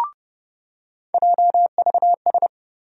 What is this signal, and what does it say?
Morse code, heard as a keyed tone, about 30 wpm, 700 Hz. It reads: JVS
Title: USA Call Sign Postfixes - 3 Letters 30wpm